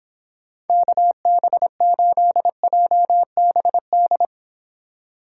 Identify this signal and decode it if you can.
Morse code, heard as a keyed tone, about 31 wpm, 700 Hz. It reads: KB8JBD